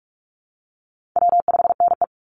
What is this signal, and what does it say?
Morse code, heard as a keyed tone, about 44 wpm, 700 Hz. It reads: W5NE
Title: Easy Call Signs Repeated 3x - 45wpm